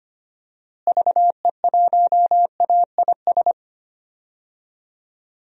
Morse code, heard as a keyed tone, about 25 wpm, 700 Hz. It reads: VE1AIS
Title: DXCC-100 Call Signs - 6 Characters - Repeated 2x - 25wpm